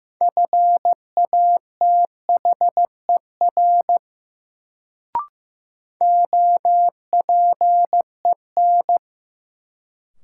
Morse code, heard as a keyed tone, about 15 wpm, 700 Hz. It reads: FATHER OPEN
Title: Sentences from Top 300 Words - New Word Drill - 15wpm